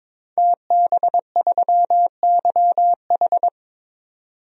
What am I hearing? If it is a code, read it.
Morse code, heard as a keyed tone, about 22 wpm, 700 Hz. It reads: TB3YH
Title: DXCC-100 Call Signs - 5 Characters - 22wpm